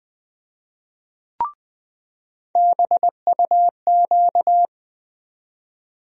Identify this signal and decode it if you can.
Morse code, heard as a keyed tone, about 20 wpm, 700 Hz. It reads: BUQ
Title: USA Call Sign Postfixes - 3 Letters 20wpm